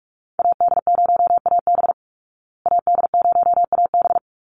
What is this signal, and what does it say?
Morse code, heard as a keyed tone, about 44 wpm, 700 Hz. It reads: AD0AB AD0AB
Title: Call Signs Repeated 3x - 45wpm